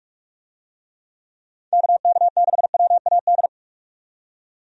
Morse code, heard as a keyed tone, about 41 wpm, 700 Hz. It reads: KK6WAD
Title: USA Call Signs - 6 Characters - 45wpm